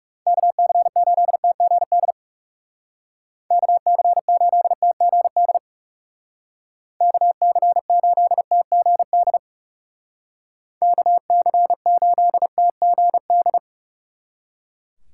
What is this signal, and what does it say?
Morse code, heard as a keyed tone, about 36 wpm, 700 Hz. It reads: KC8TGD KC8TGD KC8TGD KC8TGD
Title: Speed Racing - USA 2x3 Call Signs 30wpm